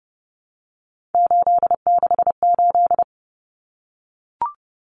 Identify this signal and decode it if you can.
Morse code, heard as a keyed tone, about 30 wpm, 700 Hz. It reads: 868
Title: Numbers - 3-Digit 30wpm